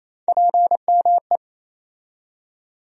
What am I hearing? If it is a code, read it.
Morse code, heard as a keyed tone, about 28 wpm, 700 Hz. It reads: PME
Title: USA Call Sign Postfixes - 3 Letters 28wpm